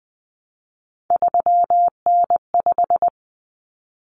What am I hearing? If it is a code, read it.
Morse code, heard as a keyed tone, about 20 wpm, 700 Hz. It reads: 3N5
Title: Common Cut Numbers - 3-Digit 20wpm